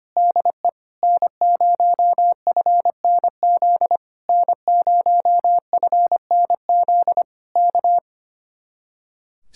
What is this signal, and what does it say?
Morse code, heard as a keyed tone, about 25 wpm, 700 Hz. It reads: DE N0FNZ N0FNZ K
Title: Speed-Racing Common QSO Phrases - 25wpm